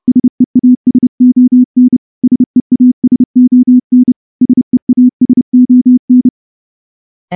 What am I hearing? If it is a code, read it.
Morse code, heard as a keyed tone, about 28 wpm, 250 Hz. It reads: SEASON SEASON SEASON
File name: morse-training-30WPM